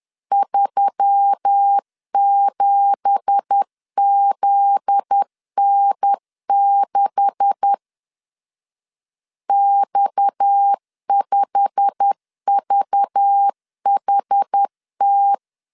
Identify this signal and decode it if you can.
Morse code, heard as a keyed tone, about 10 wpm, 800 Hz. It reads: 37ZN6 X5VHT
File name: LEZIONI / 9 2867 10wpm 2733